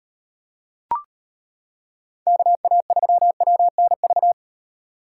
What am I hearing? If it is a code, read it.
Morse code, heard as a keyed tone, about 34 wpm, 700 Hz. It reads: KA3WNV